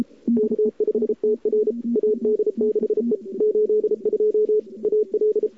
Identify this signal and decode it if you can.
Morse code, heard as a keyed tone, about 33 wpm, 440 Hz. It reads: K5TR AD6E 82 AL
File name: CWSprint / Feb2007 / 2007-feb-cw-sprint-k5tr-op-n5rz -01:00